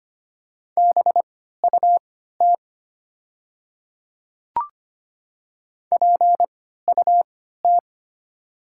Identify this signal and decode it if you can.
Morse code, heard as a keyed tone, about 25 wpm, spaced as for 14 wpm, 700 Hz. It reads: BUT PUT
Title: 3x Character Spacing - Top 500 Words - 25wpm